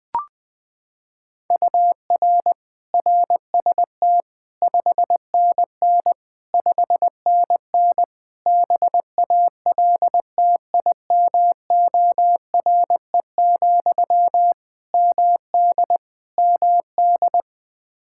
Morse code, heard as a keyed tone, about 20 wpm, 700 Hz. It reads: UR RST 5NN 5NN BALTIMORE, MD MD
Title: Practice QSO - Exchange 1 - Beginning - 20wpm